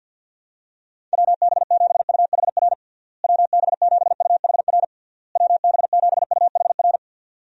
Morse code, heard as a keyed tone, about 51 wpm, 700 Hz. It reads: WB7UHR WB7UHR WB7UHR
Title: USA Call Signs - 6 Characters - Repeated 3x - 50wpm